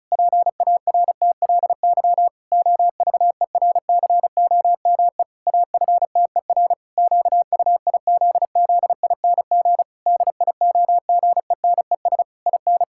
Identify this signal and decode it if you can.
Morse code, heard as a keyed tone, about 36 wpm, 700 Hz. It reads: PARTLY OVERCOME AFTER QUIZZING DIOGENES IN
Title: Pangram Challenge v2 - 35wpm